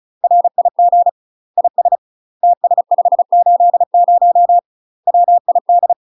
Morse code, heard as a keyed tone, about 35 wpm, 700 Hz. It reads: RIG IS TS580 WID
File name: UK MASTER QSOs TRAINING / 35WPM / UK MASTER QSOs 0030